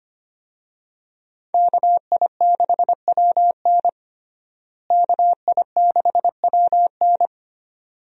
Morse code, heard as a keyed tone, about 25 wpm, 700 Hz. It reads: KI6WN KI6WN
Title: USA Call Signs - 5 Characters - Repeated 2x - 25wpm